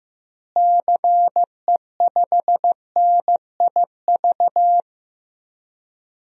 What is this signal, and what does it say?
Morse code, heard as a keyed tone, about 15 wpm, 700 Hz. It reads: CE5NIV
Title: DXCC-100 Call Signs - 6 Characters - Repeated 3x - 15wpm